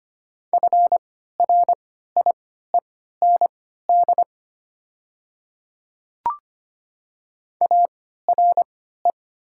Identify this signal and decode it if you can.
Morse code, heard as a keyed tone, about 25 wpm, spaced as for 14 wpm, 700 Hz. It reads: FRIEND ARE
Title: 3x Character Spacing - Top 300 Words - 25wpm